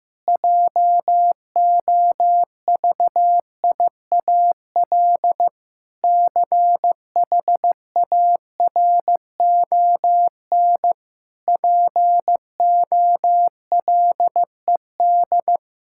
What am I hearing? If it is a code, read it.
Morse code, heard as a keyed tone, about 15 wpm, 700 Hz. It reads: JOVIAL CHARON POLED
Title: Pangram Challenge v2 - 15wpm